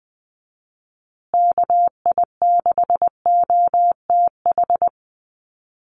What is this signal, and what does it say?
Morse code, heard as a keyed tone, about 20 wpm, 700 Hz. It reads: KI6OTH
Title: USA Call Signs Repeated 2x 20wpm